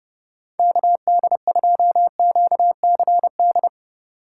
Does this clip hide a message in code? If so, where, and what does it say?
Yes: Morse code from 0.6 to 3.7 s, heard as a keyed tone, about 32 wpm, 700 Hz. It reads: KD2QCD